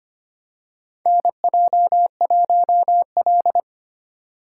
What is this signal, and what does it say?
Morse code, heard as a keyed tone, about 25 wpm, 700 Hz. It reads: NJ1L
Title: USA Call Signs - 4 Characters - 3x Character Spacing - 25wpm